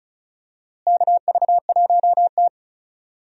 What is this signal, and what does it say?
Morse code, heard as a keyed tone, about 36 wpm, 700 Hz. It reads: KV1T
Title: USA Call Signs - 4 Characters - Repeated 3x - 35wpm